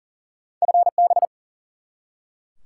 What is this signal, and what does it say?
Morse code, heard as a keyed tone, about 39 wpm, 700 Hz. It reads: FB
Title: Top 180 QSO Elements 40wpm